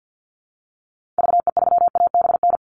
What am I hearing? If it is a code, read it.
Morse code, heard as a keyed tone, about 50 wpm, 700 Hz. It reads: VE3ABN
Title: Call Signs Repeated 2x - 50wpm